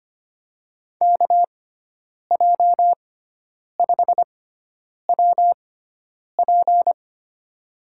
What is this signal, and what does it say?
Morse code, heard as a keyed tone, about 25 wpm, spaced as for 9 wpm, 700 Hz. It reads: KJ5WP
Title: USA Call Signs - 5 Characters - 6x Character Spacing - 25wpm